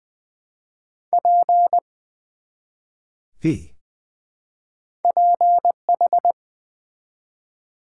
Morse code, heard as a keyed tone, about 20 wpm, 700 Hz. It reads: P PH
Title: Build A Word - 1 Letter At A Time - 3 to 5-Letter Words - 20wpm